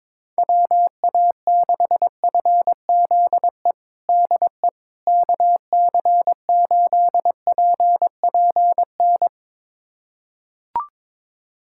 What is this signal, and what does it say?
Morse code, heard as a keyed tone, about 22 wpm, 700 Hz. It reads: WA6FZE DE KC8PPN
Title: USA Callsign DE Callsign - 6 Characters - 22wpm